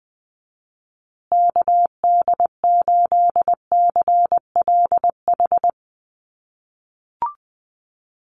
Morse code, heard as a keyed tone, about 20 wpm, 700 Hz. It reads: KD8CLH
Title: USA Call Signs Repeated 2x 20wpm